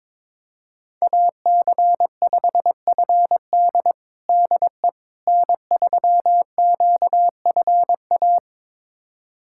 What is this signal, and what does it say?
Morse code, heard as a keyed tone, about 22 wpm, 700 Hz. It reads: AC5FD DE N3QFA
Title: USA Callsign DE Callsign - 5 Characters - 22wpm